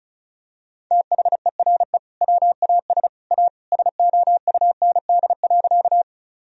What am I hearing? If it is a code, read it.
Morse code, heard as a keyed tone, about 36 wpm, 700 Hz. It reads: THERE WAS A SOUND.
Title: Sentences from Top 1000 Words - Review Part 1 - 35wpm